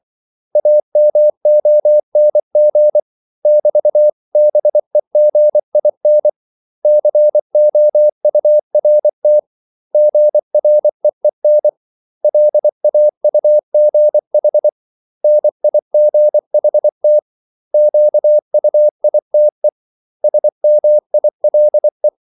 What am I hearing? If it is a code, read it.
Morse code, heard as a keyed tone, about 24 wpm, 600 Hz. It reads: AMONG <BT>BEGIN COURT GREEN LAUGH NIGHT QUITE SMILE
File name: CW / Practice / 500MostCommonEnglishWords24